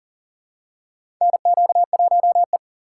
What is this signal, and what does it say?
Morse code, heard as a keyed tone, about 38 wpm, 700 Hz. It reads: NQ1E
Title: USA Call Signs - 4 Characters - 2x Character Spacing - 40wpm